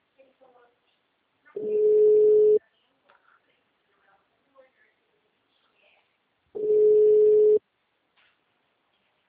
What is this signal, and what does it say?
Ringback tone (Europe, 425 Hz), with speech.